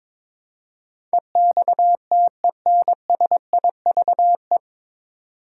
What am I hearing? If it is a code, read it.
Morse code, heard as a keyed tone, about 22 wpm, 700 Hz. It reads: EXTENSIVE
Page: Morse Code Ninja practice 22wpm